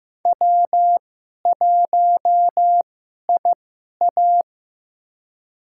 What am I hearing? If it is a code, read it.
Morse code, heard as a keyed tone, about 15 wpm, spaced as for 11 wpm, 700 Hz. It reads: W1IA
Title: USA Call Signs - 4 Characters - 2x Character Spacing - 15wpm